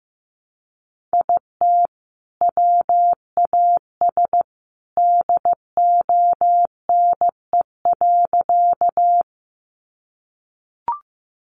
Morse code, heard as a keyed tone, about 15 wpm, 700 Hz. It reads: IT WAS DONE.
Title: Sentences from Top 500 Words 15wpm